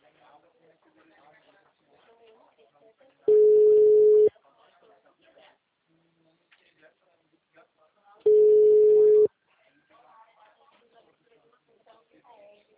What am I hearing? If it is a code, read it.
Ringback tone (Europe, 425 Hz), with speech.
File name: out- -2011- - - .87964